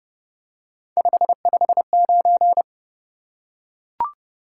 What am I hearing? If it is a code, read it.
Morse code, heard as a keyed tone, about 30 wpm, 700 Hz. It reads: 559
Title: Top 180 QSO Elements 30wpm